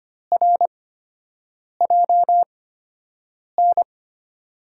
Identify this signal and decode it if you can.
Morse code, heard as a keyed tone, about 25 wpm, spaced as for 7 wpm, 700 Hz. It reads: RJN
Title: USA Call Signs - 5 Characters - 8x Character Spacing - 25wpm